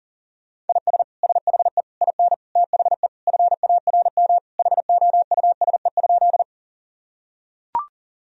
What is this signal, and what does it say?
Morse code, heard as a keyed tone, about 40 wpm, 700 Hz. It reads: IS SHE IN THE FARM HOUSE?
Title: Sentences from Top 1000 Words - Review Part 2 - 40wpm